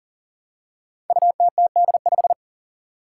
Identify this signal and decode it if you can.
Morse code, heard as a keyed tone, about 43 wpm, 700 Hz. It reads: UTTD5
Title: Five Challenging Letter-Number Combinations 40wpm